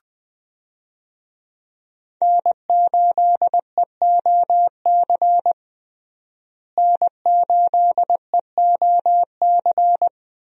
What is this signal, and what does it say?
Morse code, heard as a keyed tone, about 21 wpm, 700 Hz. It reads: N8EOC N8EOC
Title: USA Call Signs - 5 Characters - Repeated 2x - 20wpm